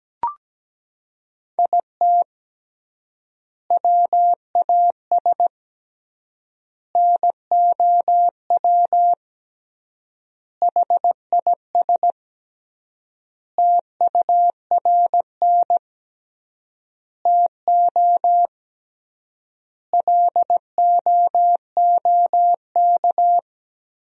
Morse code, heard as a keyed tone, about 17 wpm, 700 Hz. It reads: IT WAS NOW HIS TURN TO LOOK
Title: 3x Word-Spacing - Sentences from Top 200 Words - 17wpm